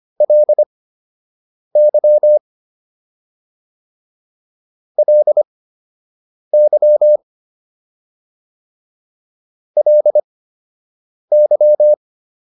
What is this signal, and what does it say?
Morse code, heard as a keyed tone, about 25 wpm, spaced as for 7 wpm, 600 Hz. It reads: LY LY LY